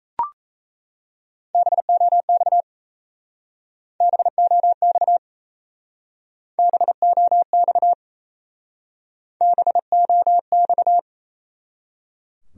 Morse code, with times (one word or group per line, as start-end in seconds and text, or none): 1.5-2.6 BOX
4.0-5.2 BOX
6.6-7.9 BOX
9.4-11.0 BOX